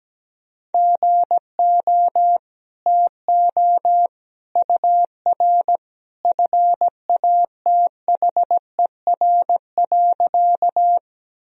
Morse code, heard as a keyed tone, about 17 wpm, 700 Hz. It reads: GO TO UR FATHER.
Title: Sentences from Top 300 Words and Abbreviations 17wpm